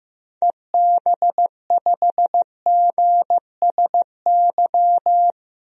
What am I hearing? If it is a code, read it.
Morse code, heard as a keyed tone, about 15 wpm, 700 Hz. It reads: EB5GSY
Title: DXCC-100 Call Signs - 6 Characters - 15wpm